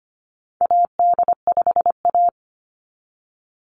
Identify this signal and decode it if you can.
Morse code, heard as a keyed tone, about 25 wpm, 700 Hz. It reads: AD5A
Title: NA SOTA Activator Callsigns 25wpm